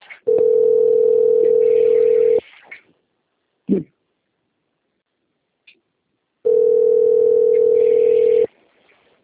Ringback tone (North America), 440 + 480 Hz.
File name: out- - - - - .67622